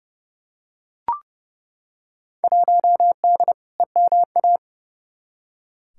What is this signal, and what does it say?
Morse code, heard as a keyed tone, about 30 wpm, 700 Hz. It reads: 1D EMA
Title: Field Day Exchanges - 30wpm